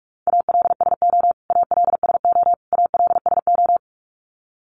Morse code, heard as a keyed tone, about 46 wpm, 700 Hz. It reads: ALSO ALSO ALSO
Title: Top 300 Words Repeated 45wpm